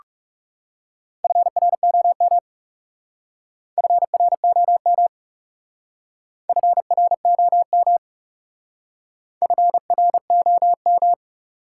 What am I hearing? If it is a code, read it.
Morse code, heard as a keyed tone, about 40 wpm, 700 Hz. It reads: FROM FROM FROM FROM